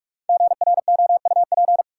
Morse code, heard as a keyed tone, about 43 wpm, 700 Hz. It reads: GROUP
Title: Sets of 3 Words - New Word Drill - 45wpm